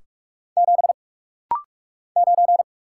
Morse code, heard as a keyed tone, about 46 wpm, 700 Hz. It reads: 7 9
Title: Single Letter-Number - Rapid-Fire 45wpm